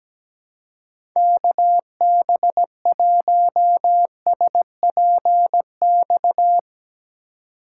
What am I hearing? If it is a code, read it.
Morse code, heard as a keyed tone, about 17 wpm, 700 Hz. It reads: KB1SPX
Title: USA Call Signs - 6 Characters - 4x Character Spacing - 17wpm